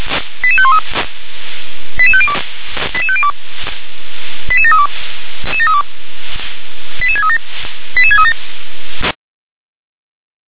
Five-tone selective call (ZVEI), heard as quick runs of five tones, tones of about 70 ms. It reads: 88513 88502 88502 88732 88637 88637
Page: SDR/ZVEI